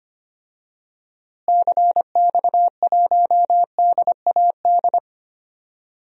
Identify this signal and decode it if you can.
Morse code, heard as a keyed tone, about 25 wpm, 700 Hz. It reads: CX1DAD
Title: DXCC-100 Call Signs - 6 Characters - Repeated 2x - 25wpm